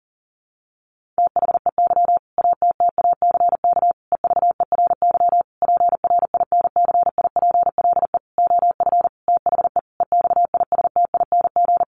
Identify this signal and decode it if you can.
Morse code, heard as a keyed tone, about 40 wpm, 700 Hz. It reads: THEY ATTACK EVERY PRINCIPLE OF THE EXISTING
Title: Sentences from Top 1500 Words 40wpm